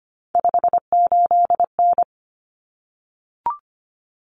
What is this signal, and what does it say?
Morse code, heard as a keyed tone, about 25 wpm, 700 Hz. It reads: 58N
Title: Common Cut Numbers - 3-Digit 25wpm